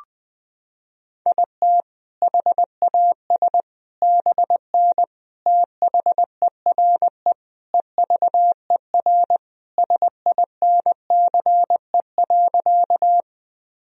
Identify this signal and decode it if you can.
Morse code, heard as a keyed tone, about 20 wpm, 700 Hz. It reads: IT HAS BN THERE EVER SINCE.
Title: Sentences from Top 400 Words and Abbreviations 20wpm